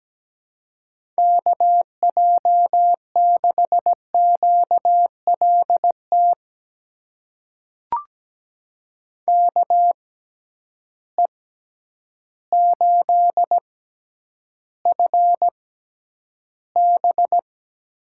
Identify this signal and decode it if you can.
Morse code, heard as a keyed tone, about 17 wpm, spaced as for 9 wpm, 700 Hz. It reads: KJ6QLT KE8FB
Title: USA Call Signs - 6x Character Spacing - 17wpm